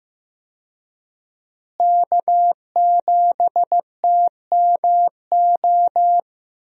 Morse code, heard as a keyed tone, about 15 wpm, 700 Hz. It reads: K7TMO
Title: USA Call Signs - 5 Characters - Repeated 2x - 15wpm